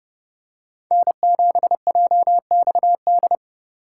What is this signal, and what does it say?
Morse code, heard as a keyed tone, about 28 wpm, 700 Hz. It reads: N7JXD